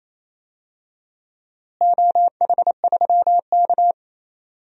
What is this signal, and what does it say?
Morse code, heard as a keyed tone, about 28 wpm, 700 Hz. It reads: OH3K
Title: DXCC-100 Call Signs - 4 Characters - Repeated 3x - 28wpm